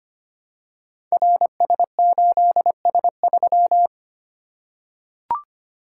Morse code, heard as a keyed tone, about 25 wpm, 700 Hz. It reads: RS8S3
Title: Code Groups - 5 Letter:Numbers - 25wpm